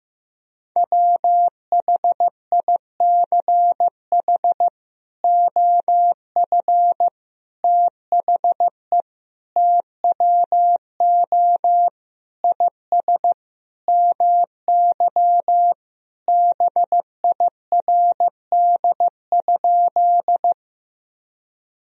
Morse code, heard as a keyed tone, about 15 wpm, 700 Hz. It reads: WHICH OF THE TWO IS MY BIRD?
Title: Sentences from Top 1000 Words - Review Part 2 - 15wpm